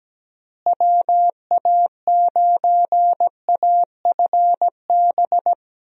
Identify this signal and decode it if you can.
Morse code, heard as a keyed tone, about 17 wpm, 700 Hz. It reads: WA9AFB
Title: USA Call Signs - 6 Characters - Repeated 3x - 17wpm